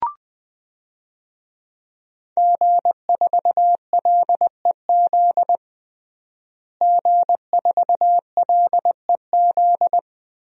Morse code, heard as a keyed tone, about 20 wpm, 700 Hz. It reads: G4LEZ G4LEZ
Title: DXCC-100 Call Signs - 5 Characters - Repeated 2x - 20wpm